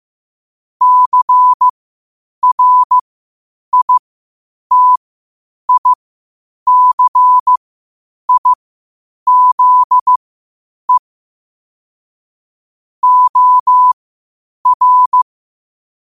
Morse code, heard as a keyed tone, about 15 wpm, spaced as for 8 wpm, 1000 Hz. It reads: CRITICIZE OR
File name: qotdfiles / 8wpm